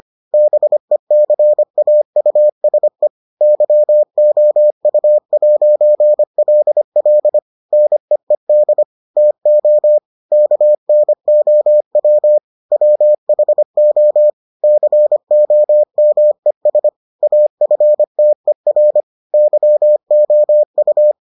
Morse code, heard as a keyed tone, about 25 wpm, 600 Hz. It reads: BECAUSE YOU'LL NEED TO KNOW WHO COMES AFTER YOU